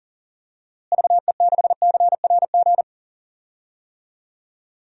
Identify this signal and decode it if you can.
Morse code, heard as a keyed tone, about 40 wpm, 700 Hz. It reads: VE6CRG
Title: DXCC-100 Call Signs - 6 Characters - Repeated 3x - 40wpm